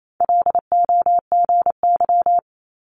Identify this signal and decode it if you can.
Morse code, heard as a keyed tone, about 28 wpm, 700 Hz. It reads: LOGY
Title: Multi-letter Phonemes 28wpm